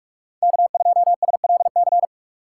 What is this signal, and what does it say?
Morse code, heard as a keyed tone, about 47 wpm, 690 Hz. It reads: K2SLC
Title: USA Call Signs - 5 Characters - 2x Character Spacing - 45wpm